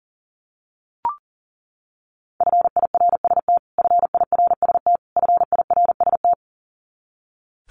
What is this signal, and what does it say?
Morse code, heard as a keyed tone, about 40 wpm, 700 Hz. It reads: FIRST FIRST FIRST
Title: Top 300 Words Repeated 40wpm